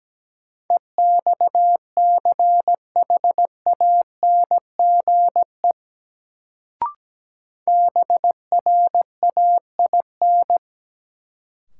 Morse code, heard as a keyed tone, about 17 wpm, 700 Hz. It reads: EXCHANGE BRAIN
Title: Sentences from Top 1000 Words - Review Part 6 - New Word Drill - 17wpm